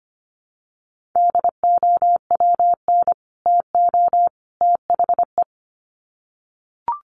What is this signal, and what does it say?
Morse code, heard as a keyed tone, about 25 wpm, 700 Hz. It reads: DOWN TO THE
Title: Sets of 3 Words 25wpm